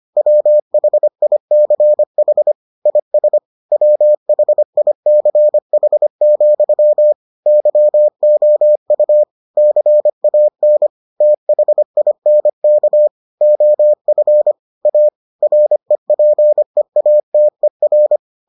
Morse code, heard as a keyed tone, about 25 wpm, 600 Hz. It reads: WHICH IS WHICH, YOU CAN THINK OF A REPEATER